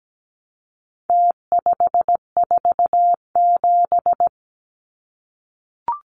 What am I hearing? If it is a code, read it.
Morse code, heard as a keyed tone, about 17 wpm, 700 Hz. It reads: T547
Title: Numbers - Contest Style - 4-Digit 17wpm